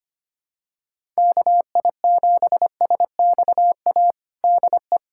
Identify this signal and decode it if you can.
Morse code, heard as a keyed tone, about 25 wpm, 700 Hz. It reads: KI7SXA DE
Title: USA Callsign DE Callsign - 6 Characters - 25wpm